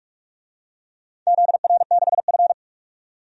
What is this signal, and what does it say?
Morse code, heard as a keyed tone, about 43 wpm, 700 Hz. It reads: ZR6F
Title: DXCC-100 Call Signs - 4 Characters - 45wpm